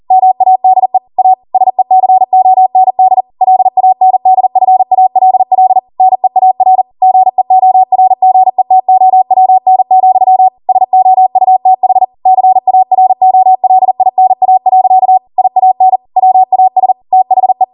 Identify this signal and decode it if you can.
Morse code, heard as a keyed tone, about 40 wpm, 750 Hz. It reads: MADE A SECOND LANDFALL NEAR GEORGETOWN, SOUTH CAROLINA. IAN WAS THE